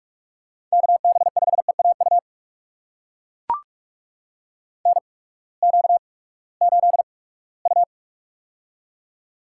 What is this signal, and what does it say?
Morse code, heard as a keyed tone, about 44 wpm, spaced as for 22 wpm, 700 Hz. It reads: KB5EAU NQ8U
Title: USA Call Signs - 8x Character Spacing - 45wpm